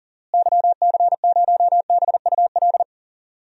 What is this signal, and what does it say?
Morse code, heard as a keyed tone, about 40 wpm, 700 Hz. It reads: YC0BUL
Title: DXCC-100 Call Signs - 6 Characters - Repeated 3x - 40wpm